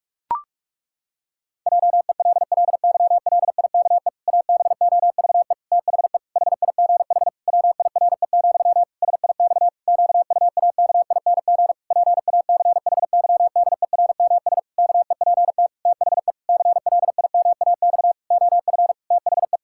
Morse code, heard as a keyed tone, about 46 wpm, 700 Hz. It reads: JELLYLIKE ABOVE THE HIGH WIRE, SIX QUAKING PACHYDERMS KEPT THE CLIMAX OF THE
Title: Pangram Challenge v2 - 45wpm